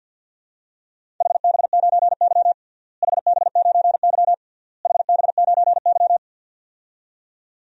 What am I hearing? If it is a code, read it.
Morse code, heard as a keyed tone, about 50 wpm, 700 Hz. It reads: HB9Y HB9Y HB9Y
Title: DXCC-100 Call Signs - 4 Characters - Repeated 3x - 50wpm